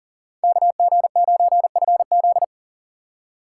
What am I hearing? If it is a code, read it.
Morse code, heard as a keyed tone, about 52 wpm, 700 Hz. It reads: KG9FZ